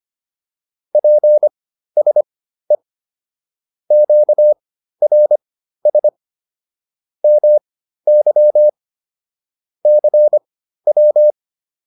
Morse code, heard as a keyed tone, about 25 wpm, spaced as for 13 wpm, 600 Hz. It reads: PSE QRS MY CW